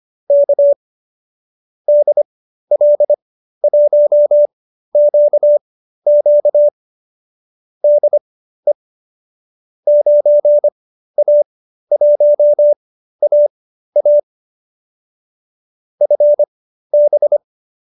Morse code, heard as a keyed tone, about 25 wpm, spaced as for 13 wpm, 590 Hz. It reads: K DL1QQ DE 9A1AA FB